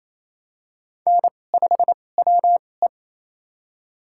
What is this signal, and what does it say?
Morse code, heard as a keyed tone, about 28 wpm, spaced as for 20 wpm, 690 Hz. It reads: N5WE
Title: USA Call Signs - 2x Character Spacing - 28wpm